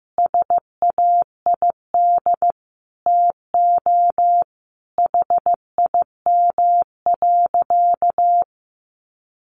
Morse code, heard as a keyed tone, about 15 wpm, 700 Hz. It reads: SAID TO HIM.
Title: Sentences from Top 800 Words 15wpm